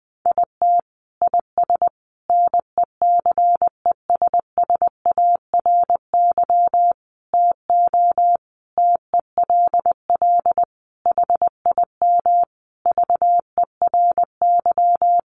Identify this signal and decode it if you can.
Morse code, heard as a keyed tone, about 20 wpm, 700 Hz. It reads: IT IS NECESSARY TO TELL HIM VERY
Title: Sentences from Top 800 Words 20wpm